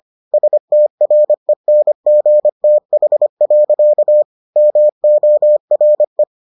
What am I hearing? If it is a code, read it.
Morse code, heard as a keyed tone, about 25 wpm, 600 Hz. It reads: STRENGTH. MORE